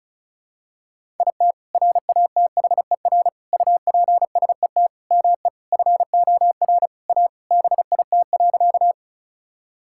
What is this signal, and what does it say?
Morse code, heard as a keyed tone, about 36 wpm, 690 Hz. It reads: IT RATHER UPSET ME FOR A BIT.
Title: Sentences from Top 1000 Words - Review Part 5 - 35wpm